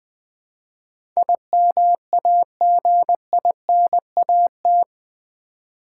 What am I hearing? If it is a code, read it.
Morse code, heard as a keyed tone, about 20 wpm, 700 Hz. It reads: IMAGINAT
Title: Build A Word - 4 Letters At A Time - 11-Letter Words - 20wpm